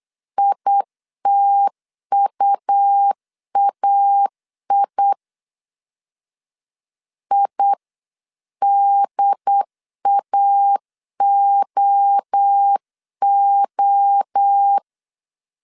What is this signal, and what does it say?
Morse code, heard as a keyed tone, about 8 wpm, 800 Hz. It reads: ITUAI IDAOO